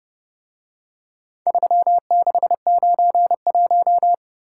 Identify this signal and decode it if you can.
Morse code, heard as a keyed tone, about 29 wpm, 700 Hz. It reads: 3691